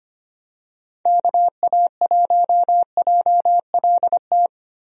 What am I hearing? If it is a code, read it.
Morse code, heard as a keyed tone, about 25 wpm, 700 Hz. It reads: KA1JLT
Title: USA Call Signs - 6 Characters - Repeated 2x - 25wpm